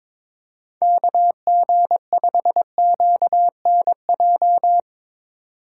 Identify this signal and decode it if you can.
Morse code, heard as a keyed tone, about 22 wpm, 700 Hz. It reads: KG5QNJ
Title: USA Call Signs - 4x Character Spacing - 22wpm